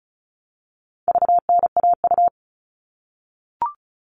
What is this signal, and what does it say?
Morse code, heard as a keyed tone, about 34 wpm, 700 Hz. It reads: VNAU
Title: All Cut Numbers - 4-Digit 35wpm